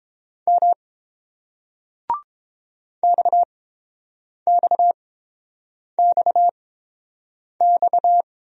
Morse code, heard as a keyed tone, about 29 wpm, spaced as for 7 wpm, 690 Hz. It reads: M XXXX